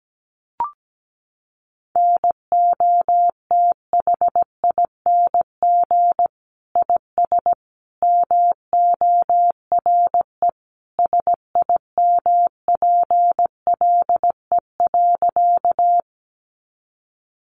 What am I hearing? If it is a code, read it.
Morse code, heard as a keyed tone, about 17 wpm, 700 Hz. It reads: NOTHING IS MORE SIMPLE.
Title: Sentences from Top 500 Words 17wpm